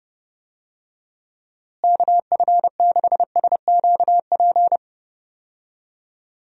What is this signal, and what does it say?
Morse code, heard as a keyed tone, about 30 wpm, 700 Hz. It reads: KF6SQP
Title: USA Call Signs - 6 Characters - Repeated 2x - 30wpm